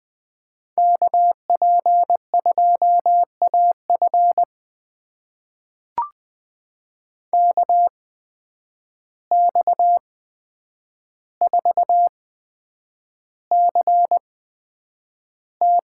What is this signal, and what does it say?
Morse code, heard as a keyed tone, about 20 wpm, spaced as for 9 wpm, 700 Hz. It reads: KP2AF KX4CT
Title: USA Call Signs - 5 Characters - 8x Character Spacing - 20wpm